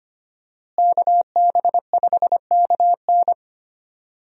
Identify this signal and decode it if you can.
Morse code, heard as a keyed tone, about 25 wpm, 700 Hz. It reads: KB5KN
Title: USA Call Signs - 5 Characters - Repeated 2x - 25wpm